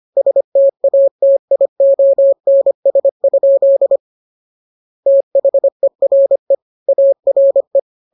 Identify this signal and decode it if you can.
Morse code, heard as a keyed tone, about 25 wpm, 550 Hz. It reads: STATIONS? THERE ARE